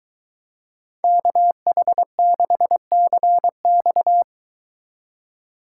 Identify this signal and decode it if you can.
Morse code, heard as a keyed tone, about 22 wpm, 700 Hz. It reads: KH6CX